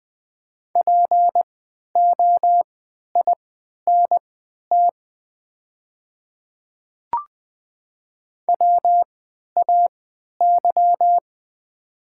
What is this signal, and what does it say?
Morse code, heard as a keyed tone, about 20 wpm, spaced as for 11 wpm, 700 Hz. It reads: POINT WAY
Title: 3x Character Spacing - Top 300 Words - 20wpm